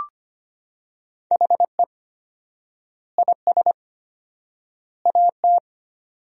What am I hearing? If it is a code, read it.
Morse code, heard as a keyed tone, about 25 wpm, 700 Hz. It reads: HE IS AT
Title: 4x Word-Spacing - Sentences from Top 300 Words - 25wpm